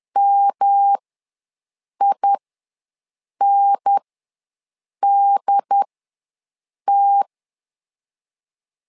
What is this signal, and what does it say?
Morse code, heard as a keyed tone, about 10 wpm, spaced as for 6 wpm, 800 Hz. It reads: MINDT
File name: LEZIONI / 3 MWRO / 3 M ws 10wpm 2121